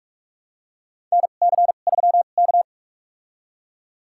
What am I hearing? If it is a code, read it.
Morse code, heard as a keyed tone, about 42 wpm, spaced as for 31 wpm, 700 Hz. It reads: NC3K
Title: USA Call Signs - 2x Character Spacing - 45wpm